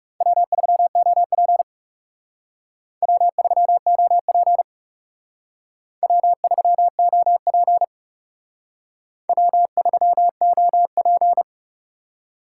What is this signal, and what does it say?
Morse code, heard as a keyed tone, about 37 wpm, 700 Hz. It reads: W3OP W3OP W3OP W3OP
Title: Speed Racing - USA Call Signs 30wpm